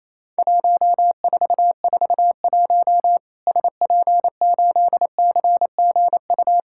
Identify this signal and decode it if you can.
Morse code, heard as a keyed tone, about 28 wpm, 700 Hz. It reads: 1441 SP8CGU
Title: Worked All Europe DX Contest - QTC Exchange - 28wpm